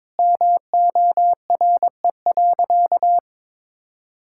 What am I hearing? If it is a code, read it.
Morse code, heard as a keyed tone, about 22 wpm, 700 Hz. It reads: MORE.
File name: practice / 22wpm / Sentences with 2-Word Phrases - 22wpm